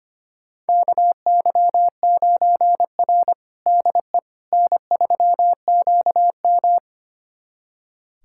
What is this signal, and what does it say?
Morse code, heard as a keyed tone, about 25 wpm, 700 Hz. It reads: KY9R DE N3QM
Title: USA Callsign DE Callsign - 4 Characters - 25wpm